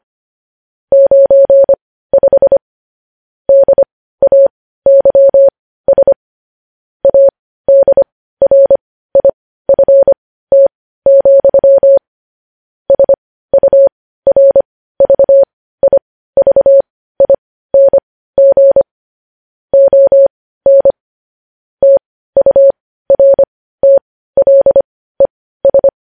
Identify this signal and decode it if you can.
Morse code, heard as a keyed tone, about 25 wpm, spaced as for 15 wpm, 550 Hz. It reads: 95 DAYS ADRIFT, SURVIVING ON TURTLES
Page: news-headlines-in-morse-code-at-15-wpm